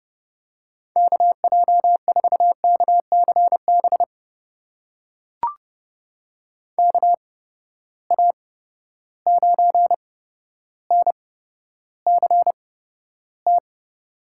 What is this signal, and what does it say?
Morse code, heard as a keyed tone, about 30 wpm, spaced as for 13 wpm, 700 Hz. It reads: KJ4KCB KA9NCT
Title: USA Call Signs - 6 Characters - 8x Character Spacing - 30wpm